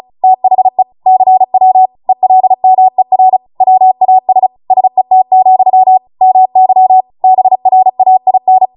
Morse code, heard as a keyed tone, about 36 wpm, 750 Hz. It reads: THE CW ELMER WAS SET, MY BRAIN